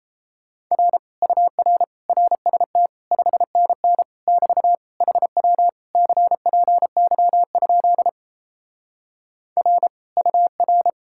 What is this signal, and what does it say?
Morse code, heard as a keyed tone, about 37 wpm, 700 Hz. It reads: R UR RST 5NN <BT> HW CPY? R UR